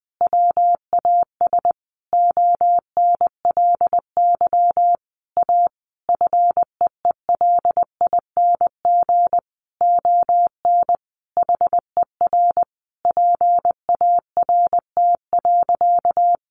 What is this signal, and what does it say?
Morse code, heard as a keyed tone, about 20 wpm, 700 Hz. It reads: WAS ONLY A FEELING ON HER PART.
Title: Sentences from Top 1200 Words 20wpm